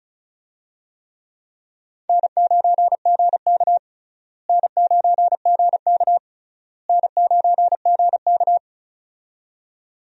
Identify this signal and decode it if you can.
Morse code, heard as a keyed tone, about 34 wpm, 700 Hz. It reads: N9GK N9GK N9GK
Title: USA Call Signs - 4 Characters - Repeated 3x - 35wpm